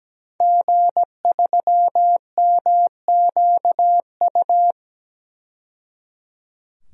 Morse code, heard as a keyed tone, about 17 wpm, 700 Hz. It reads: G3MQU
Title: DXCC-100 Call Signs - 5 Characters - 17wpm